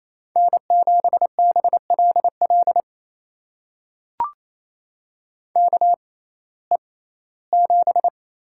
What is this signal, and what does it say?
Morse code, heard as a keyed tone, about 27 wpm, spaced as for 17 wpm, 700 Hz. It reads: N7BLL KE7
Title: USA Call Signs - 5 Characters - 6x Character Spacing - 28wpm